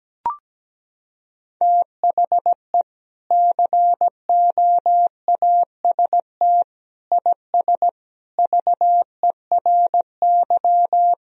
Morse code, heard as a keyed tone, about 17 wpm, 700 Hz. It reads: THE COAST IS VERY
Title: Sentences from Top 1000 Words - Review Part 4 - 17wpm